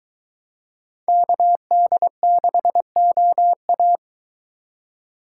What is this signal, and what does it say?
Morse code, heard as a keyed tone, about 23 wpm, 700 Hz. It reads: KD6OA